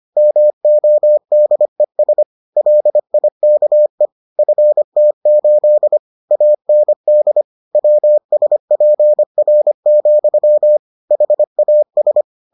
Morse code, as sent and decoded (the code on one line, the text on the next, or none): -- --- -.. ... / .-.. .. -.- . / ..-. - ---.. / .- -. -.. / .-- ... .--. .-. --..-- / .... .- ...
MODES LIKE FT8 AND WSPR, HAS